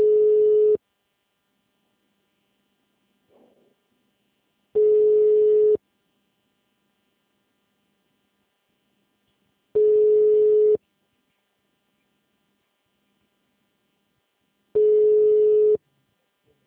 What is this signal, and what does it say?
Ringback tone (Europe, 425 Hz).
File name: out- -2024- - - .94